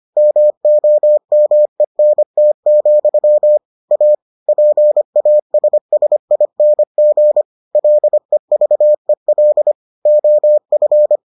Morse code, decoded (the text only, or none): MOMENT, A PASSING LEVEL OF